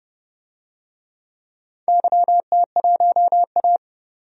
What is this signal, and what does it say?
Morse code, heard as a keyed tone, about 30 wpm, 700 Hz. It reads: YT1A
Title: DXCC-100 Call Signs - 4 Characters - 30wpm